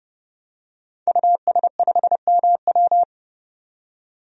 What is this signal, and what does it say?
Morse code, heard as a keyed tone, about 30 wpm, 700 Hz. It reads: US5MW
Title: DXCC-100 Call Signs - 5 Characters - 30wpm